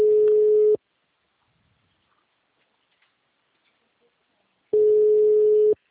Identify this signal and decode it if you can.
Ringback tone (Europe, 425 Hz).